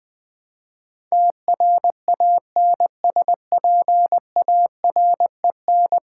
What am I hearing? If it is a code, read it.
Morse code, heard as a keyed tone, about 20 wpm, 700 Hz. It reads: TRANSPAREN
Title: Build A Word - 5 Letters At A Time - 12-Letter Words - 20wpm